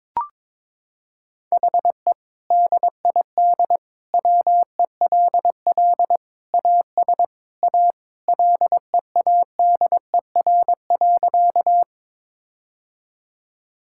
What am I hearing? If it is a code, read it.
Morse code, heard as a keyed tone, about 22 wpm, 700 Hz. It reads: HE DID WELL AS A LEADER.
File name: practice / 22wpm / Sentences with 2-Word Phrases - 22wpm